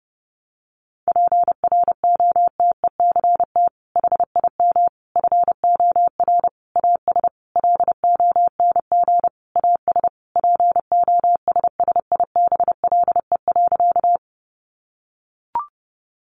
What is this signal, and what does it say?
Morse code, heard as a keyed tone, about 30 wpm, 700 Hz. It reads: PROTECT HIM FOR AS LONG AS POSSIBLE.
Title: Sentences from Top 900 Words 30wpm